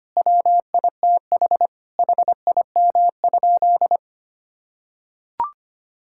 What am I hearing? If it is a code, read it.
Morse code, heard as a keyed tone, about 25 wpm, 700 Hz. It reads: WITH HIM?
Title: Sentences from Top 1000 Words - Review Part 5 - 25wpm